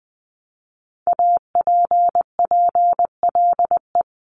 Morse code, heard as a keyed tone, about 20 wpm, 700 Hz. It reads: APPLE
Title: Multi-letter Phonemes Repeated 3x 20wpm